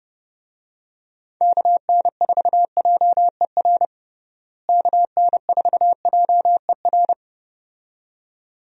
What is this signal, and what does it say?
Morse code, heard as a keyed tone, about 30 wpm, 700 Hz. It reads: KN4JER KN4JER
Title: USA Call Signs - 6 Characters - Repeated 2x - 30wpm